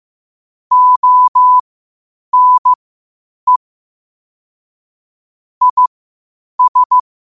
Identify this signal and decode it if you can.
Morse code, heard as a keyed tone, about 15 wpm, spaced as for 8 wpm, 1000 Hz. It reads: ONE IS
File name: qotdfiles / 8wpm